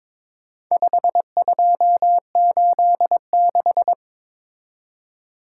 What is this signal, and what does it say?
Morse code, heard as a keyed tone, about 23 wpm, 700 Hz. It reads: 5286
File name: practice / 22wpm / Four Letter-Number Combinations - 22wpm